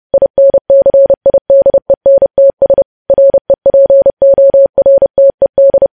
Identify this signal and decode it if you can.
Morse code, heard as a keyed tone, about 30 wpm, 550 Hz. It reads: INCIDENTS REPORTED
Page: news headlines in morse code at 30 wpm